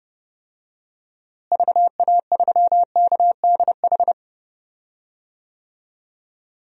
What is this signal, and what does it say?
Morse code, heard as a keyed tone, about 31 wpm, 700 Hz. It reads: VA3KDH
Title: DXCC-100 Call Signs - 6 Characters - Repeated 2x - 30wpm